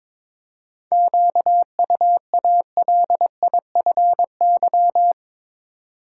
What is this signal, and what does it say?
Morse code, heard as a keyed tone, about 23 wpm, 700 Hz. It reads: QUALIFY
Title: Seven Letter Words - 22wpm